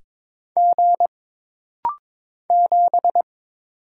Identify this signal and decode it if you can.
Morse code, heard as a keyed tone, about 22 wpm, 700 Hz. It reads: G 7
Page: This rapid-fire timing is geared to push your ICR (Instant Character Recognition) skills.